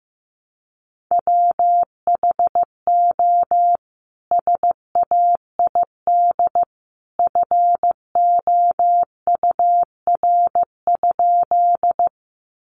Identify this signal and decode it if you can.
Morse code, heard as a keyed tone, about 15 wpm, 700 Hz. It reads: WHO SAID FOUR?
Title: Sentences from Top 300 Words 15wpm